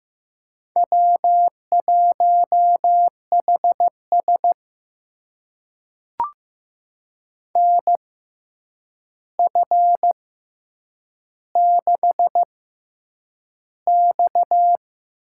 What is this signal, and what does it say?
Morse code, heard as a keyed tone, about 15 wpm, spaced as for 8 wpm, 700 Hz. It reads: W1HS NF6X
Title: USA Call Signs - 4 Characters - 6x Character Spacing - 15wpm